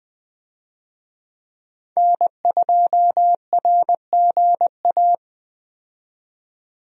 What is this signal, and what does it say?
Morse code, heard as a keyed tone, about 20 wpm, 700 Hz. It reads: N2RGA
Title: USA Call Signs - 5 Characters - 20wpm